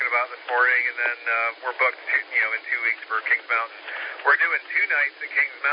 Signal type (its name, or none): Emergency Alert System (SAME) tones